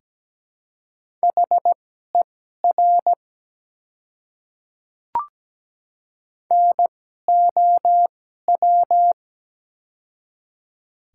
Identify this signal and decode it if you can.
Morse code, heard as a keyed tone, about 17 wpm, spaced as for 12 wpm, 700 Hz. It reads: HER NOW
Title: 2x Character Spacing - Top 300 Words - 17wpm